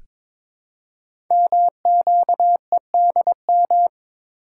Morse code, heard as a keyed tone, about 22 wpm, 700 Hz. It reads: MQEDM
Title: Code Groups - 5 Letter:Numbers - 22wpm